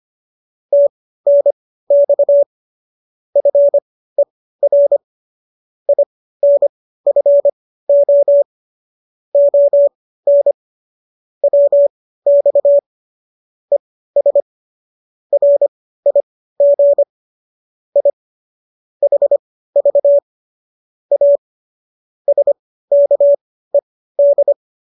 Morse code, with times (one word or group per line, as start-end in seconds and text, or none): 0.7-2.4 TNX
3.4-5.0 FER
5.9-8.4 INFO
9.3-10.5 ON
11.4-12.8 WX
13.7-14.4 ES
15.3-17.0 RIG
18.0-18.1 I
19.0-20.2 HV
21.1-21.4 A
22.3-24.5 SKED